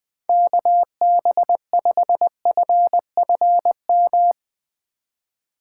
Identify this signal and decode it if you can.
Morse code, heard as a keyed tone, about 20 wpm, 700 Hz. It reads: KB5FFM
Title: USA Call Signs - 6 Characters - 3x Character Spacing - 20wpm